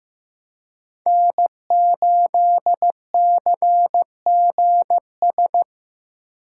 Morse code, heard as a keyed tone, about 18 wpm, 700 Hz. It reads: N8CGS
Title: Speed Racing - USA Call Signs 15wpm